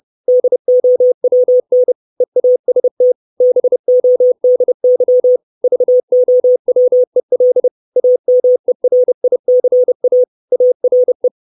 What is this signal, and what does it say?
Morse code, heard as a keyed tone, about 30 wpm, 500 Hz. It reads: DOWN EAST BODY VOWEL AMERICA ARE
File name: 632 Words 30WPM